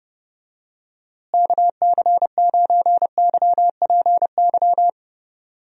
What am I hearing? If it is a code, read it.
Morse code, heard as a keyed tone, about 30 wpm, 700 Hz. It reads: KC9YPY
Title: USA Call Signs - 6 Characters - Repeated 2x - 30wpm